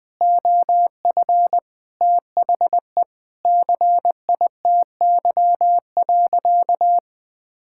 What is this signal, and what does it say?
Morse code, heard as a keyed tone, about 21 wpm, 700 Hz. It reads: OF THE CITY.
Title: Sentences with 5-Word Phrases - 20wpm